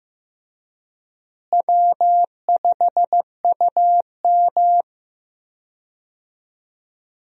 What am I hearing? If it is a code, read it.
Morse code, heard as a keyed tone, about 16 wpm, 700 Hz. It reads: W5UM
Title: USA Call Signs - 4 Characters - Repeated 2x - 15wpm